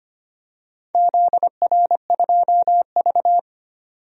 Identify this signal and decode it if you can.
Morse code, heard as a keyed tone, about 25 wpm, 700 Hz. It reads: ZR2V
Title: DXCC-100 Call Signs - 4 Characters - 25wpm